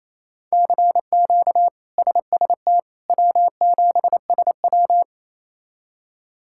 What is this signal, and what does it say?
Morse code, heard as a keyed tone, about 27 wpm, 700 Hz. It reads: CQ SST W7SW
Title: K1USN SST QSO Practice - 28wpm